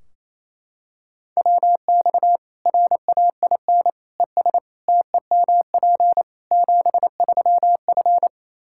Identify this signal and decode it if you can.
Morse code, heard as a keyed tone, about 28 wpm, 700 Hz. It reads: WX RAIN ES TEMP 73F
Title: Practice QSO - Exchange 2 - Info - 28wpm